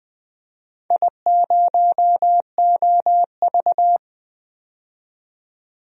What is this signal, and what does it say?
Morse code, heard as a keyed tone, about 20 wpm, 700 Hz. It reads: I0OV
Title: DXCC-100 Call Signs - 4 Characters - 20wpm